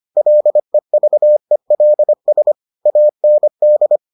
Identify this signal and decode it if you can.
Morse code, heard as a keyed tone, about 25 wpm, 600 Hz. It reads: LEVELS AND